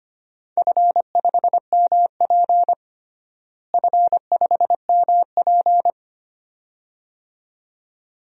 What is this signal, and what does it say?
Morse code, heard as a keyed tone, about 25 wpm, 700 Hz. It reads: F5MP F5MP
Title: DXCC-100 Call Signs - 4 Characters - Repeated 2x - 25wpm